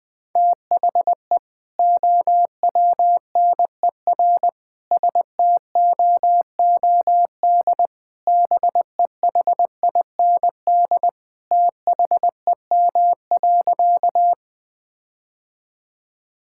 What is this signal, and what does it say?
Morse code, heard as a keyed tone, about 20 wpm, 700 Hz. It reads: THE OWNER STOOD BEHIND THEM.
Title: Sentences from Top 1600 Words 20wpm